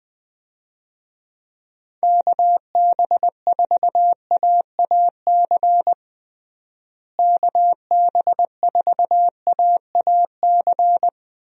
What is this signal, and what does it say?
Morse code, heard as a keyed tone, about 20 wpm, 700 Hz. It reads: KB4AAC KB4AAC
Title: USA Call Signs - 6 Characters - Repeated 2x - 20wpm